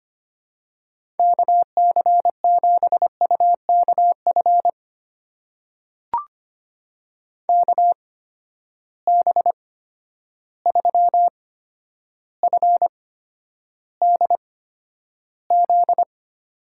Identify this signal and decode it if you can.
Morse code, heard as a keyed tone, about 25 wpm, spaced as for 11 wpm, 700 Hz. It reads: KC7UKF KB3FDZ
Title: USA Call Signs - 6 Characters - 8x Character Spacing - 25wpm